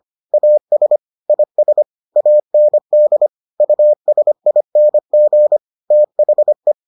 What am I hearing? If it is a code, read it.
Morse code, heard as a keyed tone, about 25 wpm, 600 Hz. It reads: AS IS AND USING THE